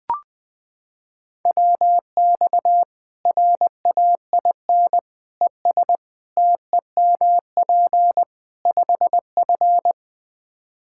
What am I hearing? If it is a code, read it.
Morse code, heard as a keyed tone, about 21 wpm, 700 Hz. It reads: WX RAIN ES TEMP 5F
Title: K1USN SST QSO Practice 20wpm